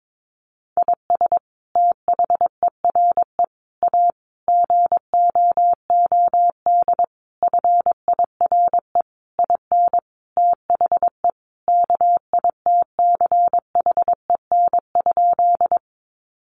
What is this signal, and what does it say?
Morse code, heard as a keyed tone, about 22 wpm, 700 Hz. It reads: IS THERE A GOOD FIRE IN THE KITCHEN?
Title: Sentences from Top 1400 Words 22wpm